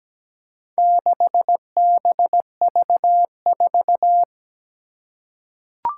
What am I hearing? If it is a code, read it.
Morse code, heard as a keyed tone, about 17 wpm, 700 Hz. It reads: 6BV4
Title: Four Challenging Letter-Number Combinations 17wpm